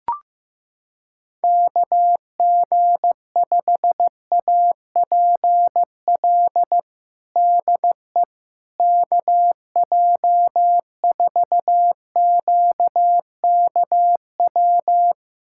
Morse code, heard as a keyed tone, about 15 wpm, 700 Hz. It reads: KG5APL DE KJ4QKW
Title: USA Callsign DE Callsign 15wpm